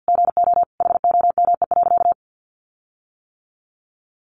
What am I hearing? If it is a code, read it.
Morse code, heard as a keyed tone, about 50 wpm, 700 Hz. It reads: GO HOME.
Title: Sentences from Top 1400 Words 50wpm